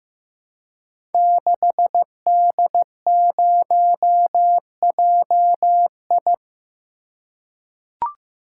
Morse code, heard as a keyed tone, about 15 wpm, 700 Hz. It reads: 6D0JI
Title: Code Groups - 5 Letter:Numbers - 15wpm